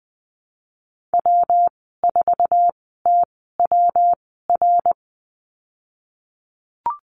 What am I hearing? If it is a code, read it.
Morse code, heard as a keyed tone, about 20 wpm, spaced as for 15 wpm, 700 Hz. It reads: W4TWR
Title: 173 Lesson 28g - Number 9 - Call Signs 20wpm